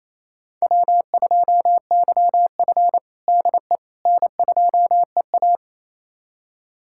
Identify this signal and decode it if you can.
Morse code, heard as a keyed tone, about 28 wpm, 700 Hz. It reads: W2YF DE N2EA
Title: USA Callsign DE Callsign - 4 Characters - 28wpm